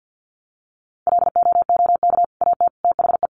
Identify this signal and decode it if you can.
Morse code, heard as a keyed tone, about 49 wpm, 700 Hz. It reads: LOOK AT THE
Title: Sets of 3 Words 50wpm